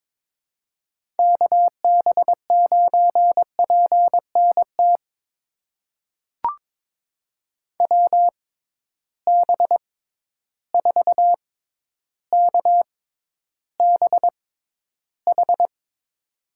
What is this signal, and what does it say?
Morse code, heard as a keyed tone, about 22 wpm, spaced as for 11 wpm, 700 Hz. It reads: KB9PNT WB4KBH
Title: USA Call Signs - 6 Characters - 6x Character Spacing - 22wpm